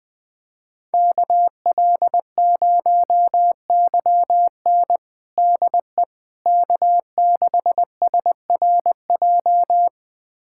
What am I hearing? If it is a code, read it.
Morse code, heard as a keyed tone, about 21 wpm, 700 Hz. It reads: KL0YN DE K6SRJ
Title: USA Callsign DE Callsign - 5 Characters - 20wpm